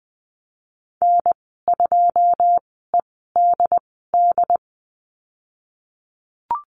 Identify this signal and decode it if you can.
Morse code, heard as a keyed tone, about 20 wpm, spaced as for 15 wpm, 700 Hz. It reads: N2EDD
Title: 088 Lesson 15g - Number 5 - Call Signs 20wpm